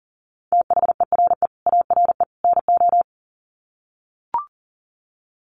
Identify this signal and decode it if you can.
Morse code, heard as a keyed tone, about 40 wpm, 700 Hz. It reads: THERE ARE NO
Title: Sets of 3 Words 40wpm